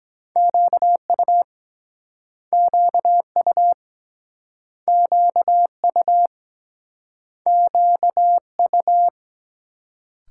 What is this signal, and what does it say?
Morse code, heard as a keyed tone, about 24 wpm, 700 Hz. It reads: QU QU QU QU